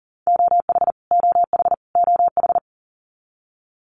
Morse code, heard as a keyed tone, about 40 wpm, 700 Hz. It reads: OH OH OH
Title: US States Repeated 3x 40wpm